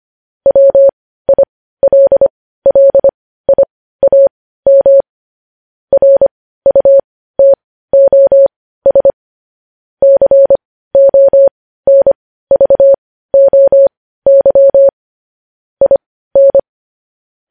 Morse code, heard as a keyed tone, about 25 wpm, spaced as for 15 wpm, 550 Hz. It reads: WILLIAM RUTOS CONVOY IN
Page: news-headlines-in-morse-code-at-15-wpm